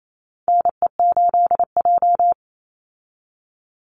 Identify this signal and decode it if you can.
Morse code, heard as a keyed tone, about 28 wpm, 700 Hz. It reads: NE8J
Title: Call Signs - 28wpm